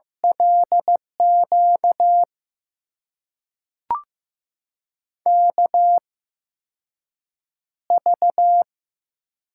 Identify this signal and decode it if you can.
Morse code, heard as a keyed tone, about 15 wpm, spaced as for 6 wpm, 700 Hz. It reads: LQ KV
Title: USA Call Signs - 5 Characters - 8x Character Spacing - 15wpm